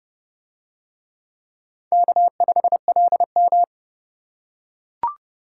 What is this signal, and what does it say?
Morse code, heard as a keyed tone, about 30 wpm, 700 Hz. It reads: K5LM